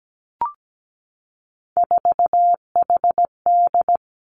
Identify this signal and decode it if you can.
Morse code, heard as a keyed tone, about 17 wpm, 700 Hz. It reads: 4HD
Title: Three Challenging Letter-Number Combinations 17wpm